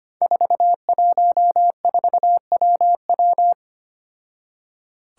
Morse code, heard as a keyed tone, about 25 wpm, 700 Hz. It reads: 414WW
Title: Five Challenging Letter-Number Combinations 25wpm